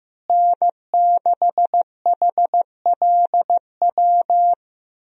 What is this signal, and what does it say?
Morse code, heard as a keyed tone, about 15 wpm, 700 Hz. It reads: N6HLW
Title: USA Call Signs - 5 Characters - Repeated 3x - 15wpm